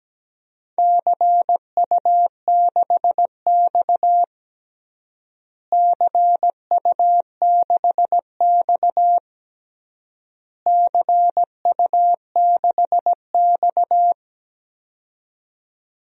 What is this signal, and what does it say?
Morse code, heard as a keyed tone, about 17 wpm, 700 Hz. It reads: CU6X CU6X CU6X
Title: DXCC-100 Call Signs - 4 Characters - Repeated 3x - 17wpm